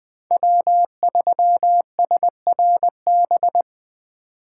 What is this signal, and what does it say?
Morse code, heard as a keyed tone, about 20 wpm, 700 Hz. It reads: W3SRB
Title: 2023 POTA Activator Call Signs - 20wpm